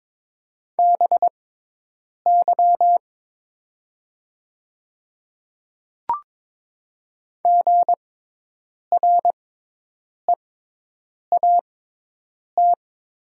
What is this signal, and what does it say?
Morse code, heard as a keyed tone, about 22 wpm, spaced as for 8 wpm, 700 Hz. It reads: BY GREAT
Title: 6x Character Spacing - Top 200 Words - 22wpm